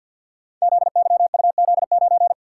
Morse code, heard as a keyed tone, about 52 wpm, 700 Hz. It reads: ZYUZ9
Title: Code Groups - 5 Letter:Numbers - 50wpm